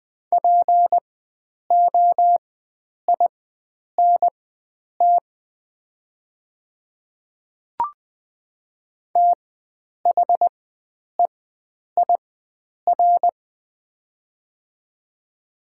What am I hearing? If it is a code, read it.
Morse code, heard as a keyed tone, about 20 wpm, spaced as for 9 wpm, 700 Hz. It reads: POINT THEIR
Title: 4x Character Spacing - Top 500 Words - 20wpm